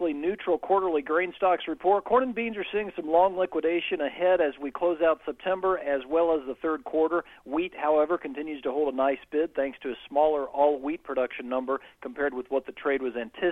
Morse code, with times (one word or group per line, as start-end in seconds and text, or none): none